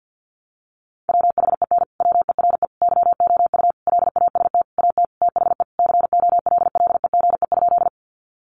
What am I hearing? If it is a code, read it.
Morse code, heard as a keyed tone, about 48 wpm, 700 Hz. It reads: WHEN WERE YOU LAST AT THE COLLEGE?
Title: Sentences from Top 1300 Words 50wpm